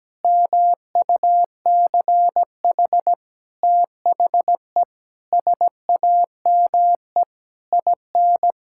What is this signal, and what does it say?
Morse code, heard as a keyed tone, about 17 wpm, 700 Hz. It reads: MUCH THE SAME IN
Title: Sentences from Top 1000 Words - Review Part 6 - 17wpm